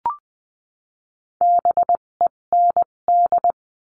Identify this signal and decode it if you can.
Morse code, heard as a keyed tone, about 20 wpm, spaced as for 17 wpm, 700 Hz. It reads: BEND
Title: 257 Lesson 42b - Words 14-20wpm